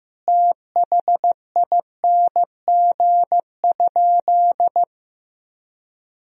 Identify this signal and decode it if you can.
Morse code, heard as a keyed tone, about 16 wpm, 700 Hz. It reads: THING?
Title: Sentences from Top 200 Words and Abbreviations 15wpm